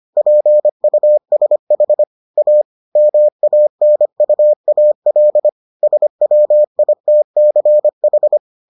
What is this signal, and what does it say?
Morse code, heard as a keyed tone, about 25 wpm, 600 Hz. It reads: PUSH A MANUAL SWITCH